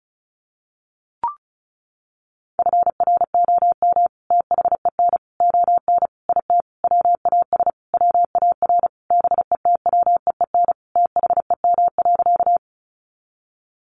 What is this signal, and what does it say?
Morse code, heard as a keyed tone, about 34 wpm, 700 Hz. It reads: FROM THEN ON IT WAS WAR BETWEEN THEM.
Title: Sentences from Top 400 Words 35wpm